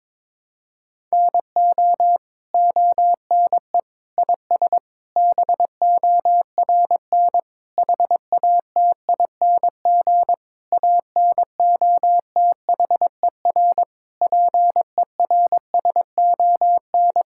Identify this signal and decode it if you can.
Morse code, heard as a keyed tone, about 22 wpm, 700 Hz. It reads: NO ONE IS BORN HATING ANOTHER PERSON
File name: Sets of 9 Words - 22wpm